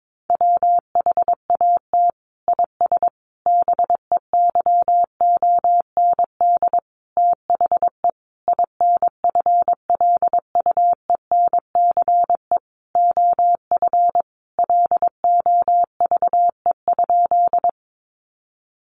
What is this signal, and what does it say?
Morse code, heard as a keyed tone, about 22 wpm, 700 Hz. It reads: WHAT IS BEYOND THE INFLUENCE OF LOVE?
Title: Sentences from Top 1100 Words 22wpm